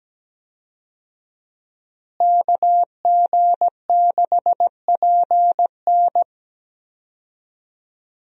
Morse code, heard as a keyed tone, about 17 wpm, 700 Hz. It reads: KG6PN
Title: USA Call Signs - 5 Characters - Repeated 3x - 17wpm